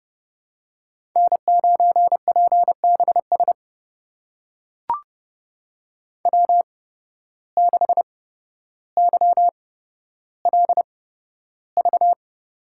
Morse code, heard as a keyed tone, about 30 wpm, spaced as for 13 wpm, 700 Hz. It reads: N9PBS W6YLV
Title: USA Call Signs - 5 Characters - 8x Character Spacing - 30wpm